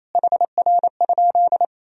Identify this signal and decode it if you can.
Morse code, heard as a keyed tone, about 28 wpm, 700 Hz. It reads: HR?
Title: Sentences from Top 200 Words and Abbreviations 28wpm